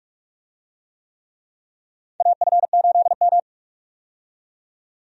Morse code, heard as a keyed tone, about 46 wpm, 700 Hz. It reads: AF8M
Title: USA Call Signs - 4 Characters - 45wpm